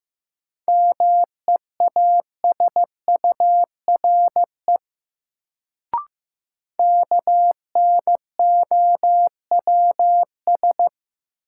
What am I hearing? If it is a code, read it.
Morse code, heard as a keyed tone, about 15 wpm, 700 Hz. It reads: MEASURE KNOWS
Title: Sentences from Top 1000 Words - Review Part 1 - New Word Drill - 15wpm